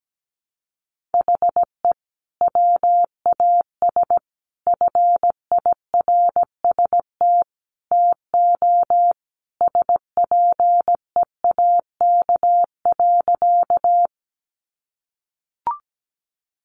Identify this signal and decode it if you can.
Morse code, heard as a keyed tone, about 17 wpm, 700 Hz. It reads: HE WAS FIRST TO SPEAK.
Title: Sentences from Top 1100 Words 17wpm